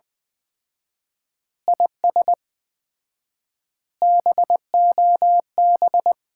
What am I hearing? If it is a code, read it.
Morse code, heard as a keyed tone, about 20 wpm, 700 Hz. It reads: IS BOB
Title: Common QSO Phrases - 4x Word-Spacing - 20wpm